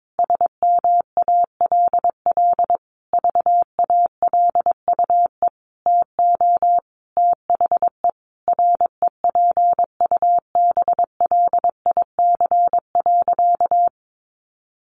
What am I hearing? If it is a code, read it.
Morse code, heard as a keyed tone, about 22 wpm, 700 Hz. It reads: SMALL VALUE TO THE REPUBLIC.
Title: Sentences from Top 1500 Words 22wpm